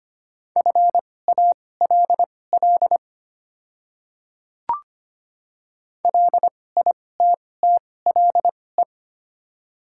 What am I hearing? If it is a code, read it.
Morse code, heard as a keyed tone, about 25 wpm, spaced as for 18 wpm, 700 Hz. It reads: FALL LITTLE
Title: 2x Character Spacing - Top 500 Words - 25wpm